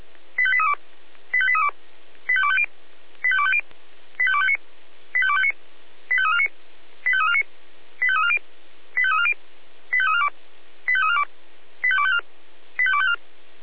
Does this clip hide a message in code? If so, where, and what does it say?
Yes: five-tone selective call (ZVEI) from 0.4 to 13.2 s, heard as quick runs of five tones, tones of about 70 ms. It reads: 85832 85832 85250 85250 85259 85259 85349 85349 85340 85340 85342 85342 85254 85254